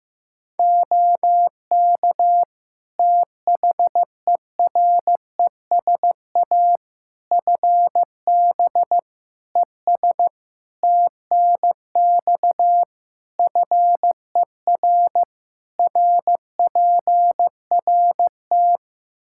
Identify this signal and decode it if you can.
Morse code, heard as a keyed tone, about 15 wpm, 700 Hz. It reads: OK THERESA FB ES TNX FER RPRT
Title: Practice QSO - Exchange 2 - Info - 15wpm